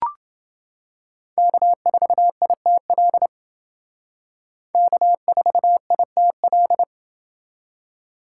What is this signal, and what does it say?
Morse code, heard as a keyed tone, about 29 wpm, 700 Hz. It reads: K4ITL K4ITL